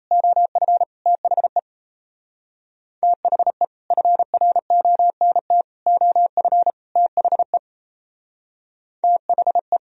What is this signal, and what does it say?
Morse code, heard as a keyed tone, about 38 wpm, 700 Hz. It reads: OF THE THE FRONT OF THE THE